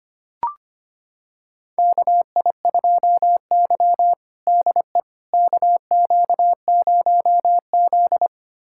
Morse code, heard as a keyed tone, about 25 wpm, 700 Hz. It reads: KI2Y DE KQ0Z
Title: USA Callsign DE Callsign - 4 Characters - 25wpm